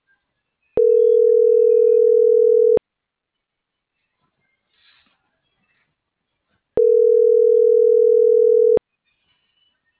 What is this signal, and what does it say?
Ringback tone (North America), 440 + 480 Hz.